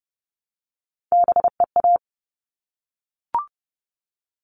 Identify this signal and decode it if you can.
Morse code, heard as a keyed tone, about 30 wpm, 700 Hz. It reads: BEA
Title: All Cut Numbers - 3-Digit 30wpm